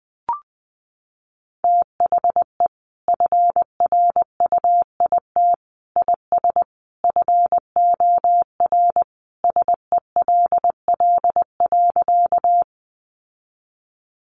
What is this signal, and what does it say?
Morse code, heard as a keyed tone, about 20 wpm, 700 Hz. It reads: THE FRUIT IS FOR SELL.
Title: Sentences from Top 1300 Words 20wpm